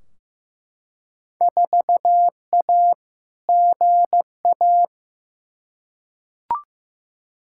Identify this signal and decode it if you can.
Morse code, heard as a keyed tone, about 15 wpm, 700 Hz. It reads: 4A GA
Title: Field Day Exchanges - 15wpm